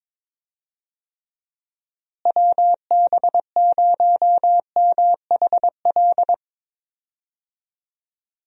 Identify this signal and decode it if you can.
Morse code, heard as a keyed tone, about 22 wpm, 700 Hz. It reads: WB0MHL
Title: USA Call Signs - 6 Characters - Repeated 2x - 22wpm